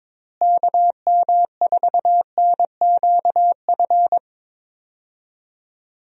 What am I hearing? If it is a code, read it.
Morse code, heard as a keyed tone, about 21 wpm, 700 Hz. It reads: KM4NQF
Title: USA Call Signs - 6 Characters - Repeated 3x - 22wpm